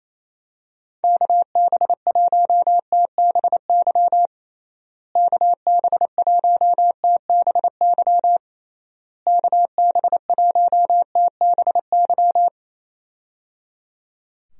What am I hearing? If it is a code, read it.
Morse code, heard as a keyed tone, about 27 wpm, 700 Hz. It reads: KB1TBY KB1TBY KB1TBY
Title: USA Call Signs - 6 Characters - Repeated 3x - 28wpm